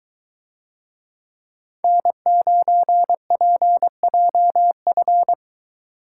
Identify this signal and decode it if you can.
Morse code, heard as a keyed tone, about 21 wpm, 700 Hz. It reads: N9PJF